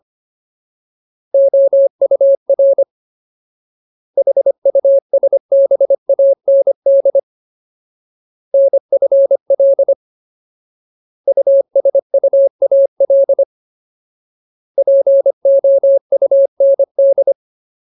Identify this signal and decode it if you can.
Morse code, heard as a keyed tone, about 25 wpm, 550 Hz. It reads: OUR HUSBAND NFL USUAL POUND
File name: Ultimate Word List / 25WPM Ultimate